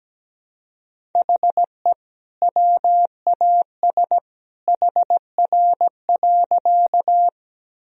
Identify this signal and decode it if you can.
Morse code, heard as a keyed tone, about 17 wpm, 700 Hz. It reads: HE WAS HR.
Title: Sentences from Top 200 Words and Abbreviations 17wpm